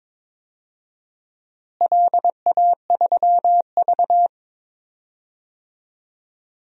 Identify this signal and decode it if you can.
Morse code, heard as a keyed tone, about 22 wpm, 700 Hz. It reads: LA3V
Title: DXCC-100 Call Signs - 4 Characters - Repeated 2x - 22wpm